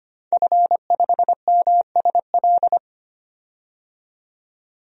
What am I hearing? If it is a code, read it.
Morse code, heard as a keyed tone, about 25 wpm, 700 Hz. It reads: F5MSL
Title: DXCC-100 Call Signs - 5 Characters - 25wpm